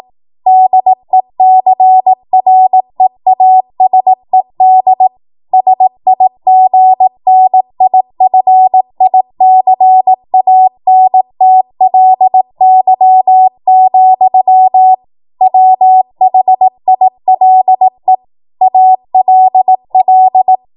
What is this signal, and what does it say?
Morse code, heard as a keyed tone, about 18 wpm, 750 Hz. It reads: DECREASED SIGNIFICANTLY, WHILE ALL
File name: Morse / 18WPM